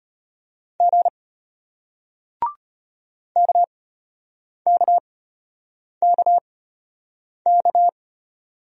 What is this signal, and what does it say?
Morse code, heard as a keyed tone, about 34 wpm, spaced as for 8 wpm, 700 Hz. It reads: G KKKK